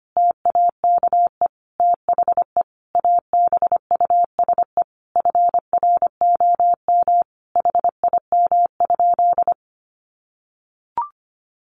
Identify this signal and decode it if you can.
Morse code, heard as a keyed tone, about 25 wpm, 700 Hz. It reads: TAKE THE ABUSE FROM HIM?
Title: Sentences from Top 1300 Words 25wpm